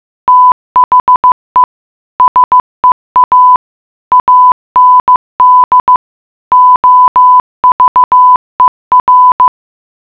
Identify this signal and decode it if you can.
Morse code, heard as a keyed tone, about 15 wpm, 1000 Hz. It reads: THE SEA AND OVER
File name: mutation.15wpm